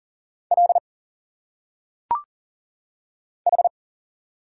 Morse code, heard as a keyed tone, about 40 wpm, 700 Hz. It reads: L H
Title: Single Letters - Phonetic Pronunciation 40wpm